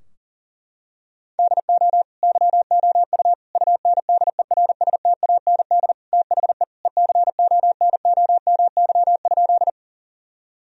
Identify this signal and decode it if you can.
Morse code, heard as a keyed tone, about 37 wpm, 700 Hz. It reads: DO YOU UNDERSTAND THE ECONOMY?
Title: Sentences from Top 1000 Words - Review Part 3 - 40wpm